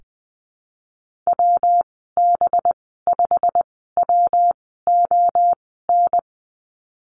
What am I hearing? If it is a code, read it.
Morse code, heard as a keyed tone, about 20 wpm, spaced as for 14 wpm, 700 Hz. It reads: WB5WON
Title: 186 Lesson 30g - Letter b - Call Signs 20wpm